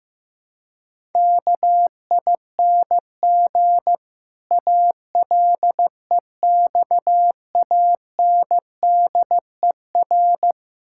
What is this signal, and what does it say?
Morse code, heard as a keyed tone, about 15 wpm, 700 Hz. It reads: KING ALEXANDER
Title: Pangram Challenge v2 - 15wpm